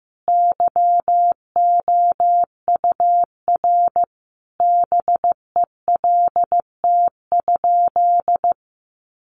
Morse code, heard as a keyed tone, about 15 wpm, 700 Hz. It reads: YOUR BELT?
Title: Sentences from Top 1400 Words 15wpm